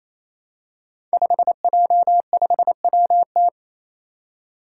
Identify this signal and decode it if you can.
Morse code, heard as a keyed tone, about 28 wpm, 700 Hz. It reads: 5J5WT
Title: Five Challenging Letter-Number Combinations 28wpm